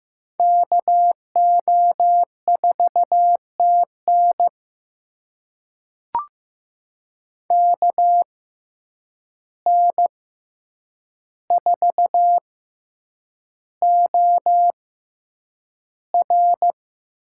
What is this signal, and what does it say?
Morse code, heard as a keyed tone, about 15 wpm, spaced as for 8 wpm, 700 Hz. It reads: KO4TN KN4OR
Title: USA Call Signs - 6x Character Spacing - 15wpm